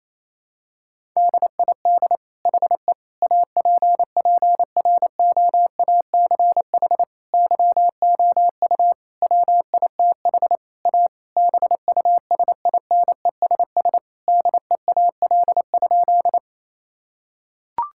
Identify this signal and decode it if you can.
Morse code, heard as a keyed tone, about 28 wpm, 700 Hz. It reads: DID HE APPROACH YOU WITH A BUSINESS DEAL?
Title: Sentences from Top 1000 Words - Review Part 6 - 28wpm